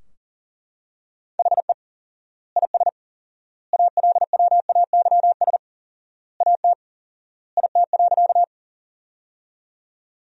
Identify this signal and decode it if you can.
Morse code, heard as a keyed tone, about 40 wpm, 700 Hz. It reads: HE IS ALWAYS AT IT.
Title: 4x Word-Spacing - Sentences from Top 300 Words - 40wpm